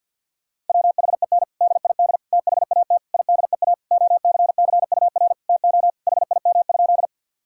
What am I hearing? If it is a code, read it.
Morse code, heard as a keyed tone, about 50 wpm, 700 Hz. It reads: WHEN DID THAT IDEA OCCUR TO HIM?
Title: Sentences from Top 1000 Words - Review Part 6 - 50wpm